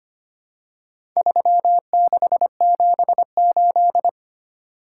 Morse code, heard as a keyed tone, about 25 wpm, 700 Hz. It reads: 3678
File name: practice / 25wpm / Four Letter-Number Combinations - 25wpm